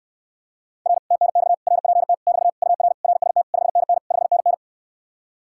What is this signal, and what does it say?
Morse code, heard as a keyed tone, about 69 wpm, 700 Hz. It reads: R TNX RPT <BT> UR RST 5NN 5NN